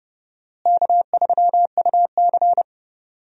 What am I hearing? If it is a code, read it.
Morse code, heard as a keyed tone, about 32 wpm, 700 Hz. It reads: K3UC
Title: USA Call Signs - 4 Characters - Repeated 2x - 30wpm